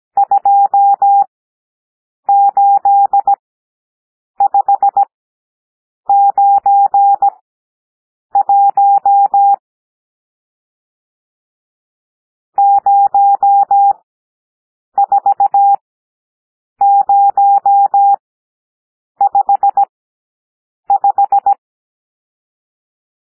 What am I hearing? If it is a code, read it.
Morse code, heard as a keyed tone, about 17 wpm, spaced as for 7 wpm, 800 Hz. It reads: 28591 04055
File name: SOUND NUMBERS 17 wpm